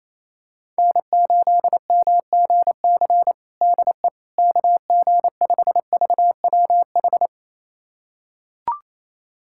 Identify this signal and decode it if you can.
Morse code, heard as a keyed tone, about 28 wpm, 700 Hz. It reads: N8MGC DE KG5VWH
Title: USA Callsign DE Callsign 28wpm